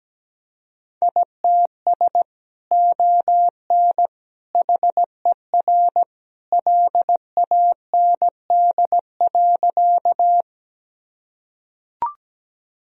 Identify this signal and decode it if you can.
Morse code, heard as a keyed tone, about 17 wpm, 700 Hz. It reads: ITS ON HER LAND.
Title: Sentences from Top 1000 Words - Review Part 1 - 17wpm